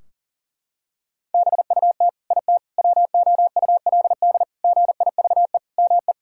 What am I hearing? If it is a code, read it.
Morse code, heard as a keyed tone, about 40 wpm, 700 Hz. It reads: BUT IT WOULD GIVE ME
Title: Sentences from Top 1600 Words 40wpm